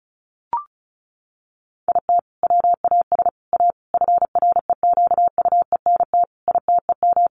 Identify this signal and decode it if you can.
Morse code, heard as a keyed tone, about 34 wpm, 700 Hz. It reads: IT WAS A FREQUENT ITEM
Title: Sentences from Top 1400 Words 35wpm